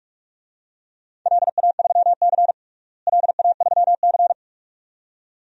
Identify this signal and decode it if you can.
Morse code, heard as a keyed tone, about 43 wpm, 700 Hz. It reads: LA3C LA3C
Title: DXCC-100 Call Signs - 4 Characters - Repeated 2x - 45wpm